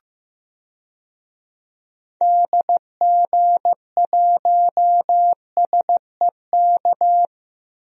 Morse code, heard as a keyed tone, about 15 wpm, 700 Hz. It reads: DG1SEK
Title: DXCC-100 Call Signs - 6 Characters - 15wpm